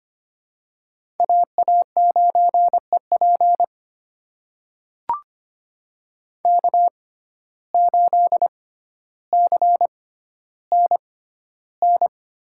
Morse code, heard as a keyed tone, about 25 wpm, spaced as for 13 wpm, 700 Hz. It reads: AA9EP K8CNN
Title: USA Call Signs - 5 Characters - 6x Character Spacing - 25wpm